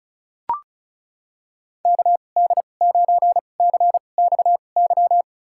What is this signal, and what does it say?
Morse code, heard as a keyed tone, about 34 wpm, spaced as for 25 wpm, 700 Hz. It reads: KD9CXY
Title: USA Call Signs - 2x Character Spacing - 35wpm